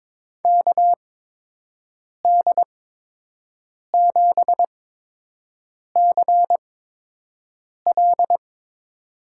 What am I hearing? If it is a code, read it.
Morse code, heard as a keyed tone, about 22 wpm, spaced as for 6 wpm, 700 Hz. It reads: KD7CL
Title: USA Call Signs - 5 Characters - 8x Character Spacing - 22wpm